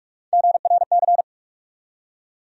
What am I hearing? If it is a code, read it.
Morse code, heard as a keyed tone, about 47 wpm, 700 Hz. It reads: GRC
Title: USA Call Sign Postfixes - 3 Letters 45wpm